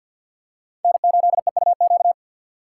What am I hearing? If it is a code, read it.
Morse code, heard as a keyed tone, about 52 wpm, 700 Hz. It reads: N8EUQ
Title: USA Call Signs - 5 Characters - 3x Character Spacing - 50wpm